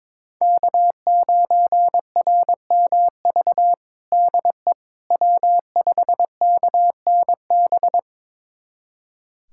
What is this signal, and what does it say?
Morse code, heard as a keyed tone, about 22 wpm, 700 Hz. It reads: K9RMV DE W5KNB
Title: USA Callsign DE Callsign - 5 Characters - 22wpm